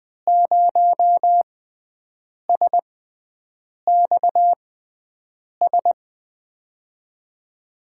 Morse code, heard as a keyed tone, about 20 wpm, spaced as for 7 wpm, 700 Hz. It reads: 0SXS
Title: USA Call Signs - 6 Characters - 6x Character Spacing - 20wpm